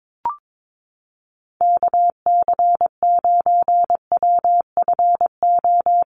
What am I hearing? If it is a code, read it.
Morse code, heard as a keyed tone, about 22 wpm, 700 Hz. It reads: KC9WFO
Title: USA Call Signs Repeated 3x 22wpm